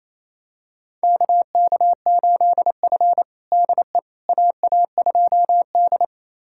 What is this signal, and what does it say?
Morse code, heard as a keyed tone, about 27 wpm, 700 Hz. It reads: KK8F DE AA2D
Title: USA Callsign DE Callsign - 4 Characters - 28wpm